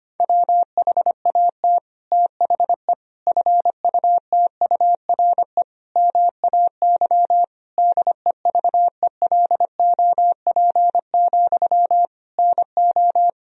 Morse code, heard as a keyed tone, about 25 wpm, 700 Hz. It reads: WHAT THE FUTURE MAY DEVELOP, NO
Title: Sentences from Top 1000 Words - Review Part 2 - 25wpm